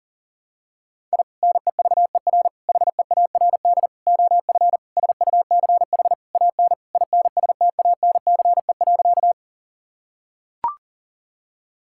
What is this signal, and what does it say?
Morse code, heard as a keyed tone, about 40 wpm, 700 Hz. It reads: I NEVER HEARD OF SUCH AN INSTANCE.
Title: Sentences from Top 1000 Words - Review Part 5 - 40wpm